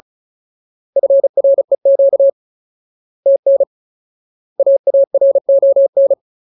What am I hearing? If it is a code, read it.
Morse code, heard as a keyed tone, about 34 wpm, 550 Hz. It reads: FREQ TN AARON
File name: Ultimate Word List / 35WPM Ultimate